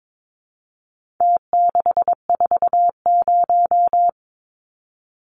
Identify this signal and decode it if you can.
Morse code, heard as a keyed tone, about 22 wpm, 700 Hz. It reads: T640
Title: Numbers - Contest Style - 4-Digit 22wpm